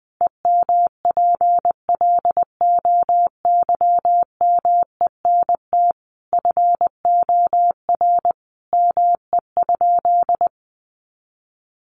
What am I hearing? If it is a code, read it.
Morse code, heard as a keyed tone, about 19 wpm, 700 Hz. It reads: EMPLOYMENT FOR ME?
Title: Sentences from Top 1500 Words 20wpm